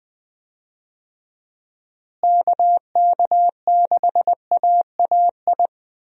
Morse code, heard as a keyed tone, about 20 wpm, 700 Hz. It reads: KK6AAI
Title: USA Call Signs - 6 Characters - Repeated 3x - 20wpm